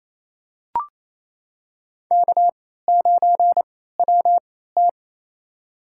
Morse code, heard as a keyed tone, about 28 wpm, spaced as for 16 wpm, 700 Hz. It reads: K9WT
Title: USA Call Signs - 3x Character Spacing - 28wpm